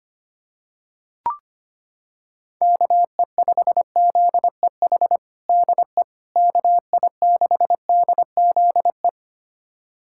Morse code, heard as a keyed tone, about 25 wpm, 700 Hz. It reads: KE5ZEH DE KI6DZE
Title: USA Callsign DE Callsign 25wpm